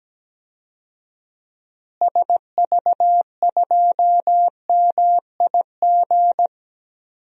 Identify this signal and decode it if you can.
Morse code, heard as a keyed tone, about 17 wpm, 700 Hz. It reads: SV2MIG
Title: DXCC-100 Call Signs - 6 Characters - Repeated 3x - 17wpm